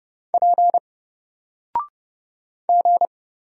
Morse code, heard as a keyed tone, about 30 wpm, 700 Hz. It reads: P G
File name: Single-Letter-Number-30wpm